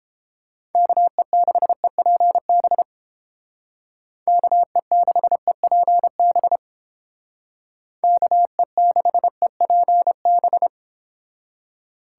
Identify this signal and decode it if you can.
Morse code, heard as a keyed tone, about 29 wpm, 700 Hz. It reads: KE6EPB KE6EPB KE6EPB